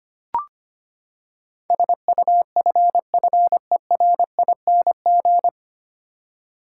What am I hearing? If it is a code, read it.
Morse code, heard as a keyed tone, about 25 wpm, 700 Hz. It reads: SUFFERING
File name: Nine Letter Words - 25wpm